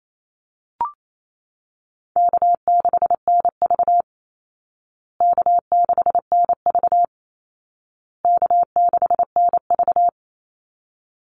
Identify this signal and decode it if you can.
Morse code, heard as a keyed tone, about 28 wpm, 700 Hz. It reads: K6NV K6NV K6NV
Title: Easy Call Signs Repeated 3x - 28wpm